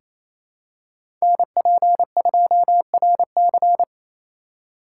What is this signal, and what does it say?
Morse code, heard as a keyed tone, about 28 wpm, 700 Hz. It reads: NP2RC
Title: USA Call Signs - 5 Characters - 4x Character Spacing - 28wpm